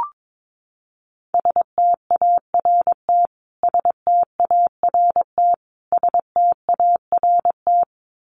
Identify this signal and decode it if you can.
Morse code, heard as a keyed tone, about 22 wpm, 700 Hz. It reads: START START START
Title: Top 500 Words Repeated 22wpm